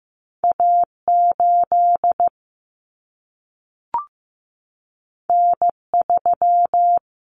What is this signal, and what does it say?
Morse code, heard as a keyed tone, about 15 wpm, 700 Hz. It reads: A 8 N 3
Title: Common Cut Numbers - 2-Digit 15wpm